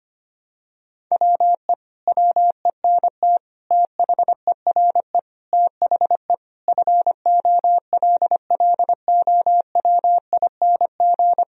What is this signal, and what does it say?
Morse code, heard as a keyed tone, about 25 wpm, 700 Hz. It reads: WE WENT THERE THE FOLLOWING
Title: Sentences from Top 1000 Words - Review Part 4 - 25wpm